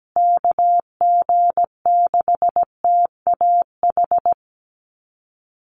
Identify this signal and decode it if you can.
Morse code, heard as a keyed tone, about 17 wpm, 700 Hz. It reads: KG6TAH
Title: USA Call Signs Repeated 3x 17wpm